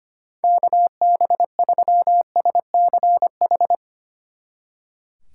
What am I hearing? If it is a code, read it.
Morse code, heard as a keyed tone, about 33 wpm, 700 Hz. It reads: KB3SCH